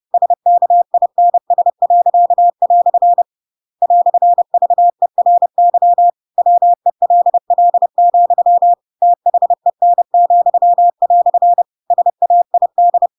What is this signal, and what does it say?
Morse code, heard as a keyed tone, about 30 wpm, 700 Hz. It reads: SKINS." "VERY WELL, THEN," SAID
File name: Pinocchio / Pinocchio ch7 30 30 ik1hge